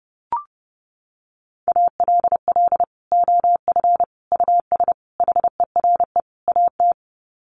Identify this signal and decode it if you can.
Morse code, heard as a keyed tone, about 30 wpm, 700 Hz. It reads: ALL OF US HERE AT
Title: Sets of 5 Words 30wpm